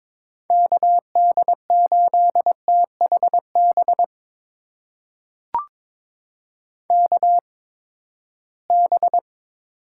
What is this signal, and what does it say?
Morse code, heard as a keyed tone, about 21 wpm, spaced as for 15 wpm, 700 Hz. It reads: KD8THB KB
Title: USA Call Signs - 6 Characters - 8x Character Spacing - 22wpm